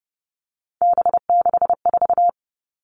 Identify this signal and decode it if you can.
Morse code, heard as a keyed tone, about 30 wpm, 700 Hz. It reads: B64
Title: All Cut Numbers - 3-Digit 30wpm